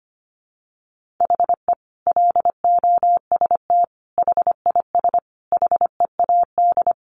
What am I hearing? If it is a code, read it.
Morse code, heard as a keyed tone, about 25 wpm, 700 Hz. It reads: HE LOST HIS HEAD
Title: Sentences from Top 1400 Words 25wpm